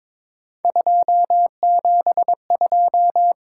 Morse code, heard as a keyed tone, about 22 wpm, 700 Hz. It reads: 272
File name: practice / 22wpm / Three Letter-Number Combinations - 22wpm